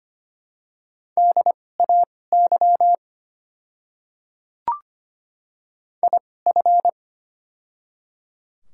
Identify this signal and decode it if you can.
Morse code, heard as a keyed tone, about 25 wpm, spaced as for 18 wpm, 690 Hz. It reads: DAY IF
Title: 2x Character Spacing - Top 100 Words - 25wpm